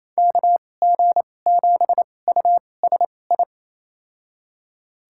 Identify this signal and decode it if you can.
Morse code, heard as a keyed tone, about 28 wpm, spaced as for 20 wpm, 700 Hz. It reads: KG7USI
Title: USA Call Signs - 6 Characters - 2x Character Spacing - 28wpm